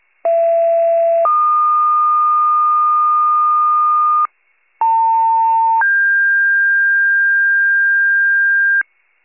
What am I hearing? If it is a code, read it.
Two-tone page, heard as a one-second tone, then a three-second tone, about 670 then 1220 Hz, about 900 then 1600 Hz, with noise. It is heard from 0.2 to 8.8 s.